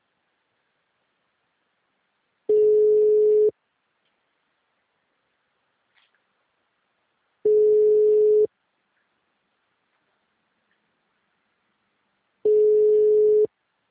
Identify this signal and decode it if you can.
Ringback tone (Europe, 425 Hz).
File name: out- -2005- - - .723